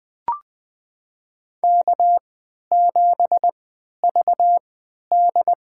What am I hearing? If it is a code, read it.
Morse code, heard as a keyed tone, about 20 wpm, spaced as for 11 wpm, 700 Hz. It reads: K7VD
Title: USA Call Signs - 4 Characters - 3x Character Spacing - 20wpm